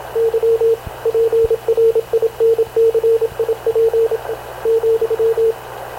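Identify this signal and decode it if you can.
Morse code, heard as a keyed tone, about 26 wpm, 460 Hz, with radio noise. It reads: Y PRINCIP ,
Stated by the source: Havana Radio / CLA sending Spanish press (PX) in Morse, recorded by Radio Officer Paul Dumesnil onboard CNAV QUEST/CZDO.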